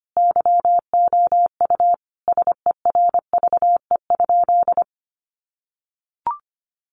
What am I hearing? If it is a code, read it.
Morse code, heard as a keyed tone, about 25 wpm, 700 Hz. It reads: YOU SERVE?
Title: Sentences from Top 400 Words 25wpm